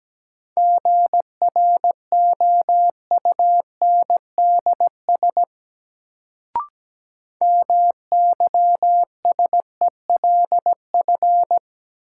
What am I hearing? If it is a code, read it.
Morse code, heard as a keyed tone, about 17 wpm, 700 Hz. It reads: GROUNDS MYSELF
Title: Sentences from Top 1300 Words - New Word Drill - 17wpm